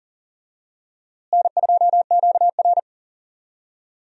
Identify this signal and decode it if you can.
Morse code, heard as a keyed tone, about 40 wpm, 700 Hz. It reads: N2QR
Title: USA Call Signs - 4 Characters - Repeated 3x - 40wpm